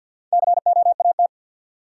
Morse code, heard as a keyed tone, about 48 wpm, 700 Hz. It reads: COAT
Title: Four Letter Words - 50wpm